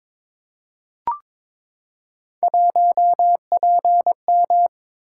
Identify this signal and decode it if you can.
Morse code, heard as a keyed tone, about 22 wpm, 700 Hz. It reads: 1PM
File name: practice / 22wpm / Three Letter-Number Combinations - 22wpm